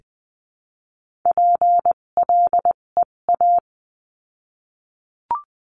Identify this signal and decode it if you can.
Morse code, heard as a keyed tone, about 20 wpm, spaced as for 17 wpm, 700 Hz. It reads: PLEA
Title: 257 Lesson 42b - Words 14-20wpm